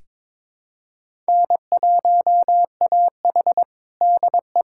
Morse code, heard as a keyed tone, about 22 wpm, 700 Hz. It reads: N1AH DE
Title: USA Callsign DE Callsign - 4 Characters - 22wpm